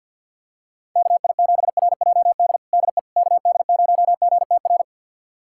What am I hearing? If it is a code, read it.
Morse code, heard as a keyed tone, about 52 wpm, 700 Hz. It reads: KI7RJD DE KD0GTR
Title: USA Callsign DE Callsign - 6 Characters - 50wpm